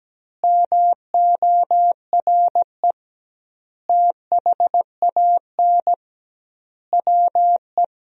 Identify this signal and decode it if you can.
Morse code, heard as a keyed tone, about 17 wpm, 700 Hz. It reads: MORE THAN WE
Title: 2x Word-Spacing - Sentences from Top 100 Words - 17wpm